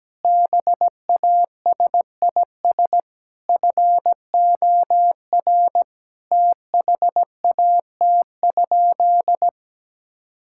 Morse code, heard as a keyed tone, about 17 wpm, 700 Hz. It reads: BASIS FOR THAT?
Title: Sentences from Top 1000 Words - Review Part 5 - 17wpm